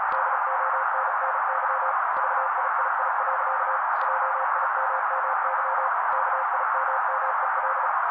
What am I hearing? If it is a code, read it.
Morse code, heard as a keyed tone, about 36 wpm, 530 Hz. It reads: DONNY VIII. JIMMY KIMMEL
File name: daily / 35wpm noise